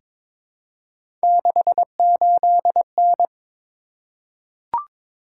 Morse code, heard as a keyed tone, about 22 wpm, 700 Hz. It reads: 68N
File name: practice / 22wpm / Three Letter-Number Combinations - 22wpm